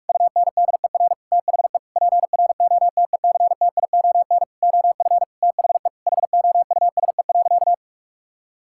Morse code, heard as a keyed tone, about 41 wpm, 700 Hz. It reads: UNDER THE PROTECTION OF THE HOUSE.
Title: Sentences from Top 1600 Words 45wpm